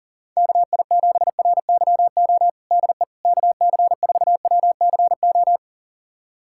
Morse code, heard as a keyed tone, about 40 wpm, 700 Hz. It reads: KI7RYO DE KC4WCO
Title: USA Callsign DE Callsign - 6 Characters - 40wpm